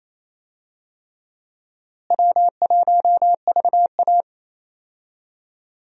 Morse code, heard as a keyed tone, about 28 wpm, 700 Hz. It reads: W1VA
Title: USA Call Signs - 4 Characters - Repeated 3x - 28wpm